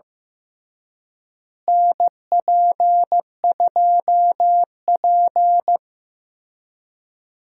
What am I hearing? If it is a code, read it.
Morse code, heard as a keyed tone, about 15 wpm, 700 Hz. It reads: NP2P
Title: USA Call Signs - 4 Characters - Repeated 3x - 15wpm